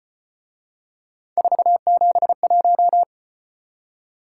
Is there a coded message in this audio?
Yes: Morse code from 1.4 to 3.0 s, heard as a keyed tone, about 34 wpm, 700 Hz. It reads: 471